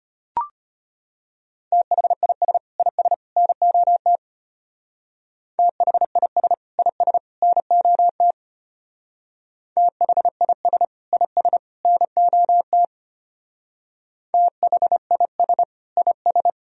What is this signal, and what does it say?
Morse code, heard as a keyed tone, about 31 wpm, 700 Hz. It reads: THIS IS NOT THIS IS NOT THIS IS NOT THIS IS